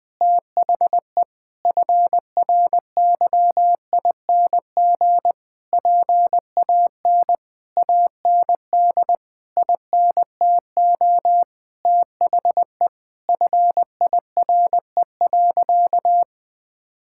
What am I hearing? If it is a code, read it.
Morse code, heard as a keyed tone, about 20 wpm, 700 Hz. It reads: THE FRYING PAN AND INTO THE FIRE.
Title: Sentences with 9-Word Phrases - 20wpm